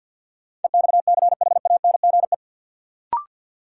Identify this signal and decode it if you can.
Morse code, heard as a keyed tone, about 53 wpm, 700 Hz. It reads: EXCHANGE
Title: Sentences from Top 1000 Words - Review Part 6 - New Word Drill - 50wpm